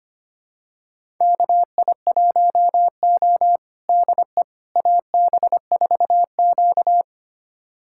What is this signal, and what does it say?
Morse code, heard as a keyed tone, about 25 wpm, 700 Hz. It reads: KI1O DE AB4Q
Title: USA Callsign DE Callsign - 4 Characters - 25wpm